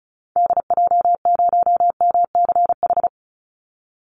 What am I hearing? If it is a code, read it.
Morse code, heard as a keyed tone, about 34 wpm, 700 Hz. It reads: DJ0MCH
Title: Call Signs Repeated 3x - 35wpm